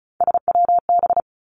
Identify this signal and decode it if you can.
Morse code, heard as a keyed tone, about 35 wpm, 700 Hz. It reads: SWB
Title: Three Challenging Letter-Number Combinations 35wpm